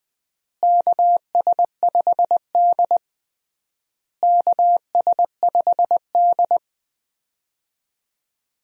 Morse code, heard as a keyed tone, about 20 wpm, 700 Hz. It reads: KS5D KS5D
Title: USA Call Signs - 4 Characters - Repeated 3x - 20wpm